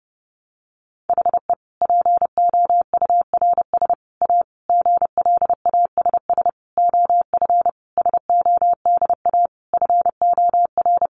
Morse code, heard as a keyed tone, about 30 wpm, 700 Hz. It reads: HE POURS A GLASS OF SODA FOR
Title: Additional Easy Phrases 30wpm